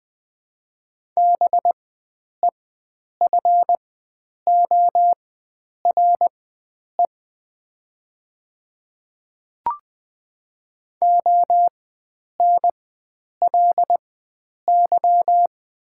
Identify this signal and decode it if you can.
Morse code, heard as a keyed tone, about 20 wpm, spaced as for 9 wpm, 700 Hz. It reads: BEFORE ONLY
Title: 4x Character Spacing - Top 300 Words - 20wpm